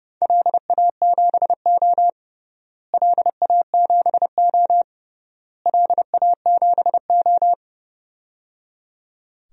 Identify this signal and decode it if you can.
Morse code, heard as a keyed tone, about 30 wpm, 700 Hz. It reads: LA7O LA7O LA7O
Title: DXCC-100 Call Signs - 4 Characters - Repeated 3x - 30wpm